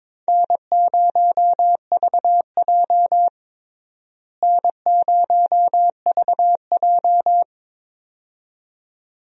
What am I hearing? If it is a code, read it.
Morse code, heard as a keyed tone, about 22 wpm, 700 Hz. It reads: N0VJ N0VJ
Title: USA Call Signs - 4 Characters - Repeated 2x - 22wpm